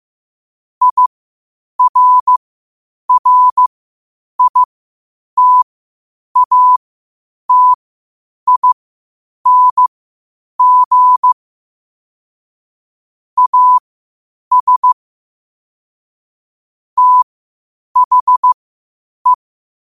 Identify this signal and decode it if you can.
Morse code, heard as a keyed tone, about 15 wpm, spaced as for 8 wpm, 1000 Hz. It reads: IRRITATING AS THE
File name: qotdfiles / 8wpm